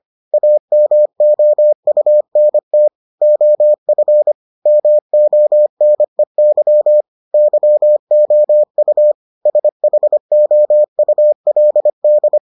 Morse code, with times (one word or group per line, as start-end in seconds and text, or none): 0.3-2.9 AMOUNT
3.2-4.3 OF
4.7-7.0 MONEY
7.3-9.1 YOU
9.5-12.4 SHOULD